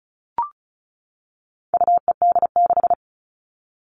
Morse code, heard as a keyed tone, about 34 wpm, 700 Hz. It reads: UED6
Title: All Cut Numbers - 4-Digit 35wpm